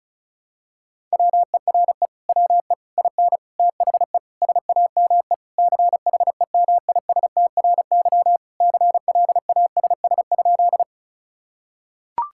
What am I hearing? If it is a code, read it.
Morse code, heard as a keyed tone, about 36 wpm, 700 Hz. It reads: WERE WE IN THE SAME CHEMISTRY CLASS?
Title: Sentences from Top 1000 Words - Review Part 6 - 35wpm